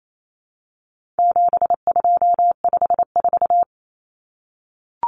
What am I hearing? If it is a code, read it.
Morse code, heard as a keyed tone, about 28 wpm, 700 Hz. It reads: 7254
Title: Common Cut Numbers - 4-Digit 28wpm